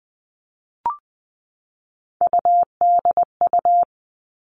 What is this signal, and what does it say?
Morse code, heard as a keyed tone, about 20 wpm, 700 Hz. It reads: UDU
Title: Three Challenging Letter-Number Combinations 20wpm